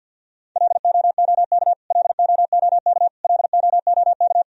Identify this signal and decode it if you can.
Morse code, heard as a keyed tone, about 52 wpm, 700 Hz. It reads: LOOK LOOK LOOK
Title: Top 100 Words Repeated 50wpm